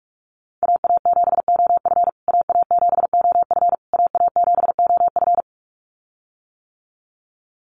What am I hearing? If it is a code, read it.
Morse code, heard as a keyed tone, about 44 wpm, 700 Hz. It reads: AA7OF AA7OF AA7OF
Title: USA Call Signs Repeated 3x 45wpm